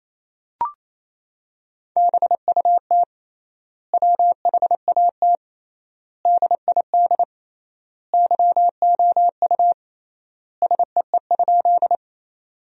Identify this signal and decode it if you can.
Morse code, heard as a keyed tone, about 28 wpm, 700 Hz. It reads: BUT WHAT DID YOU SEE?
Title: 3x Word-Spacing - Sentences from Top 100 Words - 28wpm